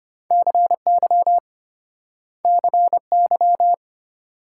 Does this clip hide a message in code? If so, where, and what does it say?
Yes: Morse code from 0.3 to 3.7 s, heard as a keyed tone, about 33 wpm, 700 Hz. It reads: CY CY